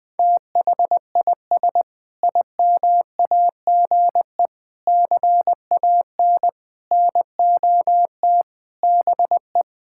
Morse code, heard as a keyed tone, about 21 wpm, 700 Hz. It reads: THIS IMAGE CAN NOT BE
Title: Sentences with 5-Word Phrases - 20wpm